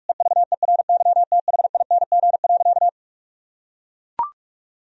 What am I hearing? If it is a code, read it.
Morse code, heard as a keyed tone, about 41 wpm, 700 Hz. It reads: EVERYTHING.
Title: Sentences from Top 1000 Words - Review Part 1 - 45wpm